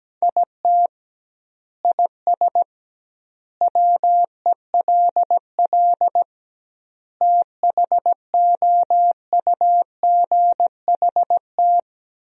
Morse code, heard as a keyed tone, about 17 wpm, 700 Hz. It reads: IT IS WELL THOUGHT
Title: 2x Word-Spacing - Sentences from Top 300 Words - 17wpm